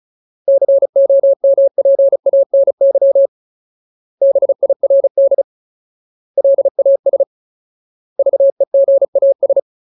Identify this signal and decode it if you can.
Morse code, heard as a keyed tone, about 35 wpm, 550 Hz. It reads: COMPANY BIRD LAS VEGAS
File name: Ultimate Word List / 35WPM Ultimate